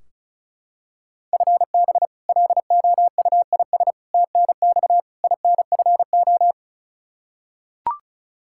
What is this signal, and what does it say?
Morse code, heard as a keyed tone, about 34 wpm, 700 Hz. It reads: FB LOUIS TNX INFO
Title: Practice QSO - Exchange 3 - More Info - 35wpm